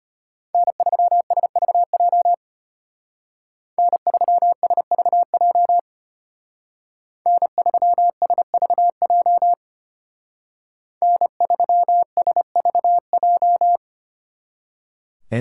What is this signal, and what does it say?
Morse code, heard as a keyed tone, about 33 wpm, 700 Hz. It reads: N3SVJ N3SVJ N3SVJ N3SVJ